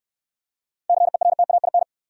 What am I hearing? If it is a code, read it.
Morse code, heard as a keyed tone, about 68 wpm, 700 Hz. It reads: CERTAIN